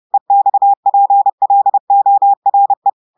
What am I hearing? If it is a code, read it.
Morse code, heard as a keyed tone, about 30 wpm, 800 Hz. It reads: EXPLORE
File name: portugais texte en clair 2 vitesse 30 mots minutes